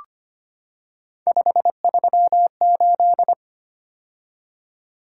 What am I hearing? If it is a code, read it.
Morse code, heard as a keyed tone, about 25 wpm, 700 Hz. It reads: 538
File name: Three Letter-Number Combinations - 25wpm